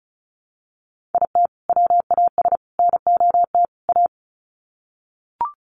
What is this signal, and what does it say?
Morse code, heard as a keyed tone, about 34 wpm, 700 Hz. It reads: IT WAS NOT A
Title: Sets of 4 Words 35wpm